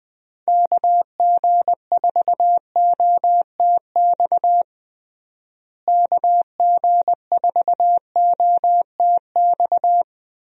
Morse code, heard as a keyed tone, about 20 wpm, 700 Hz. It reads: KG4OTX KG4OTX
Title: USA Call Signs - 6 Characters - Repeated 3x - 20wpm